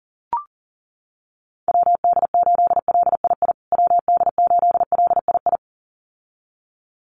Morse code, heard as a keyed tone, about 40 wpm, 700 Hz. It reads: WD8LII WD8LII
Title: USA Call Signs Repeated 2x 40wpm